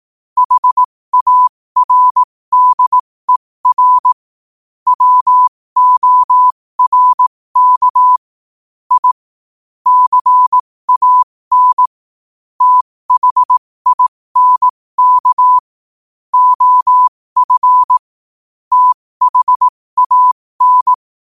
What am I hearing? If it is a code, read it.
Morse code, heard as a keyed tone, about 18 wpm, spaced as for 15 wpm, 1000 Hz. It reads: HARDER WORK I CAN THINK OF THAN
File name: 15wpm